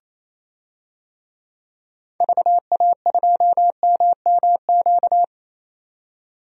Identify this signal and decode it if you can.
Morse code, heard as a keyed tone, about 28 wpm, 700 Hz. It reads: VA2MMQ
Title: DXCC-100 Call Signs - 6 Characters - 28wpm